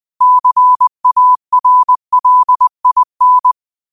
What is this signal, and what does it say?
Morse code, heard as a keyed tone, about 20 wpm, 1000 Hz. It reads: CARLIN
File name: qotdfiles / 20wpm